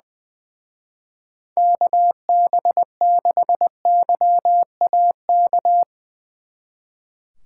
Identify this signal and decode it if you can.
Morse code, heard as a keyed tone, about 22 wpm, 700 Hz. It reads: KB6YAK
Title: Speed Racing - USA Call Signs 20wpm